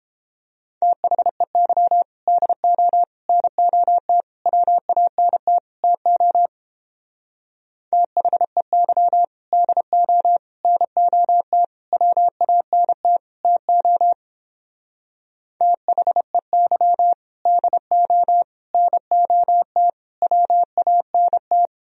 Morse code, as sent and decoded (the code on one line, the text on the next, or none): - .... -.-- / -.. --- / -. --- - / .-- .- -. - / - --- / - .... -.-- / -.. --- / -. --- - / .-- .- -. - / - --- / - .... -.-- / -.. --- / -. --- - / .-- .- -. -
THEY DO NOT WANT TO THEY DO NOT WANT TO THEY DO NOT WANT